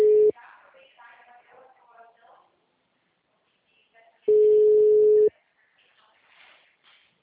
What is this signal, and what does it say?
Ringback tone (Europe, 425 Hz).
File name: out- -2011- - - .8073